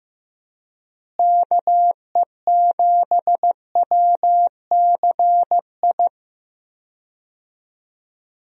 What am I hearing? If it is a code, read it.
Morse code, heard as a keyed tone, about 16 wpm, 700 Hz. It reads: KE7WCI
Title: USA Call Signs - 6 Characters - Repeated 3x - 15wpm